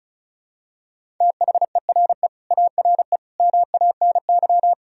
Morse code, heard as a keyed tone, about 36 wpm, 700 Hz. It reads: THERE ARE MANY
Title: Sentences from Top 1000 Words - Review Part 1 - 35wpm